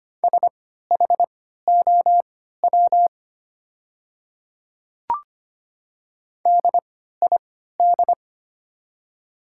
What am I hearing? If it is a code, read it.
Morse code, heard as a keyed tone, about 25 wpm, spaced as for 14 wpm, 700 Hz. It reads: SHOW DID
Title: 3x Character Spacing - Top 300 Words - 25wpm